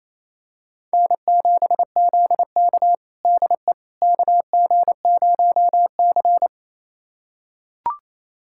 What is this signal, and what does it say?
Morse code, heard as a keyed tone, about 28 wpm, 700 Hz. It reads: N7ZK DE KG0C
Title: USA Callsign DE Callsign - 4 Characters - 28wpm